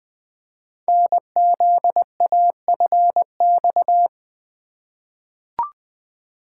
Morse code, heard as a keyed tone, about 20 wpm, 700 Hz. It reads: NZAFX
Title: Code Groups - 5 Letters - 20wpm